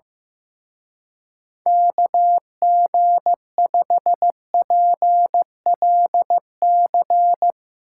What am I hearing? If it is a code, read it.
Morse code, heard as a keyed tone, about 16 wpm, 700 Hz. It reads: KG5PLC
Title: Speed Racing - USA 2x3 Call Signs 15wpm